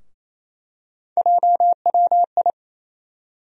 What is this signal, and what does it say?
Morse code, heard as a keyed tone, about 28 wpm, 700 Hz. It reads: JWI
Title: USA Call Sign Postfixes - 3 Letters 28wpm